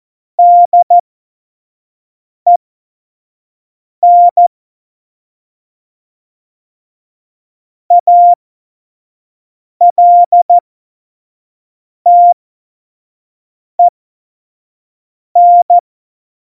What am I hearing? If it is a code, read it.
Morse code, heard as a keyed tone, about 14 wpm, spaced as for 5 wpm, 700 Hz. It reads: DEN ALTEN
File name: Morsekurs / Witze / Witz 05. Juni 2018, 25 F70